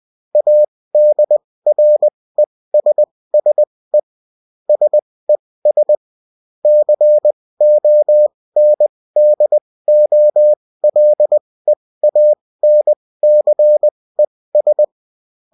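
Morse code, heard as a keyed tone, about 20 wpm, spaced as for 16 wpm, 600 Hz. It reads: ADRESSE SES CONDOLEANCES
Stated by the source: Bulletin in text and CW form.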